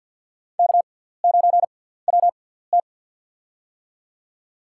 Morse code, heard as a keyed tone, about 47 wpm, spaced as for 17 wpm, 700 Hz. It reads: K9WT
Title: USA Call Signs - 4 Characters - 6x Character Spacing - 50wpm